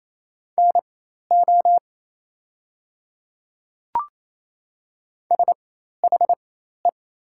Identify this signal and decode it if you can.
Morse code, heard as a keyed tone, about 27 wpm, spaced as for 13 wpm, 700 Hz. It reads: NO SHE
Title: 4x Character Spacing - Top 100 Words - 28wpm